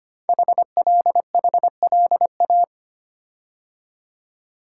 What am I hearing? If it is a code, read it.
Morse code, heard as a keyed tone, about 25 wpm, 700 Hz. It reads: HLHLA
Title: Code Groups - 5 Letters - 25wpm